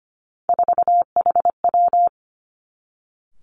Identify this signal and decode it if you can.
Morse code, heard as a keyed tone, about 25 wpm, 700 Hz. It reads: 4HW
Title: Three Challenging Letter-Number Combinations 25wpm